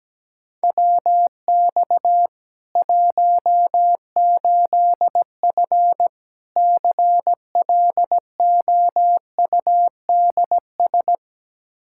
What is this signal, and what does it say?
Morse code, heard as a keyed tone, about 17 wpm, 700 Hz. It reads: WX 18F CLOUDS
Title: Practice QSO - Exchange 2 - Info - 17wpm